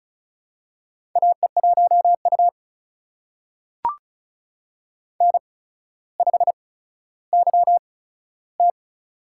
Morse code, heard as a keyed tone, about 36 wpm, spaced as for 15 wpm, 700 Hz. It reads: AE1U N5YT
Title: USA Call Signs - 4 Characters - 8x Character Spacing - 35wpm